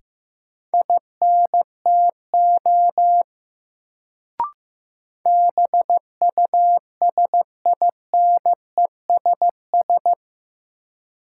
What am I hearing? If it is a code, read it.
Morse code, heard as a keyed tone, about 15 wpm, 700 Hz. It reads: INTO BUSINESS
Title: Sentences from Top 600 Words - New Word Drill - 15wpm